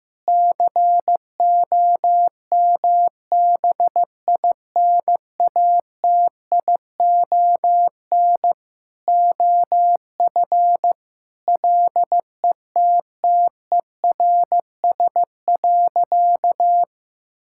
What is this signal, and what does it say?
Morse code, heard as a keyed tone, about 15 wpm, 700 Hz. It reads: COMBINATION OF LETTERS.
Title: Sentences from Top 1000 Words - Review Part 6 - 15wpm